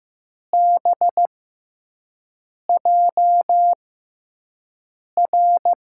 Morse code, heard as a keyed tone, about 15 wpm, spaced as for 5 wpm, 700 Hz. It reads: BJR
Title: USA Call Signs - 6x Character Spacing - 15wpm